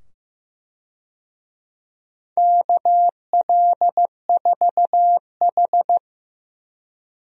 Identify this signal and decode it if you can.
Morse code, heard as a keyed tone, about 15 wpm, 700 Hz. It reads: KL4H
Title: USA Call Signs - 4 Characters - 15wpm